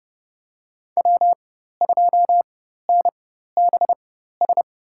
Morse code, heard as a keyed tone, about 30 wpm, spaced as for 14 wpm, 700 Hz. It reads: W2NBS
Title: USA Call Signs - 4x Character Spacing - 30wpm